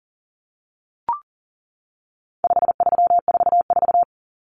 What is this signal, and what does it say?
Morse code, heard as a keyed tone, about 40 wpm, 700 Hz. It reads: 5344
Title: Common Cut Numbers - 4-Digit 40wpm